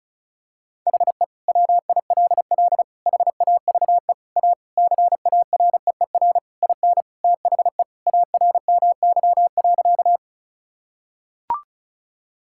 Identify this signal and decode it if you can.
Morse code, heard as a keyed tone, about 34 wpm, 700 Hz. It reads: HE WILL HAVE A CAREER IN THE ARMY.
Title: Sentences from Top 1000 Words - Review Part 6 - 35wpm